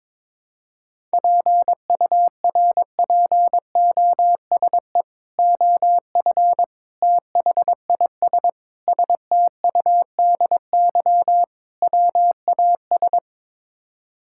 Morse code, heard as a keyed tone, about 22 wpm, 700 Hz. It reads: PURPOSE OF THIS STUDY WAS
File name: practice / 22wpm / Build A Phrase - 5-Word Phrases - 22wpm